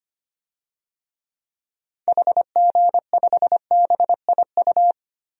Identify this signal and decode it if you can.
Morse code, heard as a keyed tone, about 25 wpm, 700 Hz. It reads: HG5BIU
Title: DXCC-100 Call Signs - 6 Characters - Repeated 3x - 25wpm